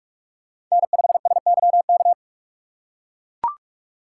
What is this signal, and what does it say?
Morse code, heard as a keyed tone, about 42 wpm, 700 Hz. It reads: N5SYK